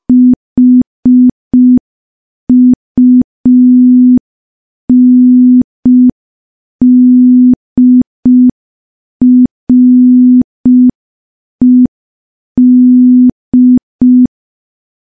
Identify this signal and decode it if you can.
Morse code, heard as a keyed tone, about 5 wpm, 250 Hz. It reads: HUNDRED
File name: morse / morse-training-05WPM